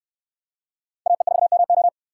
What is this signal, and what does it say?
Morse code, heard as a keyed tone, about 63 wpm, 700 Hz. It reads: AE4MO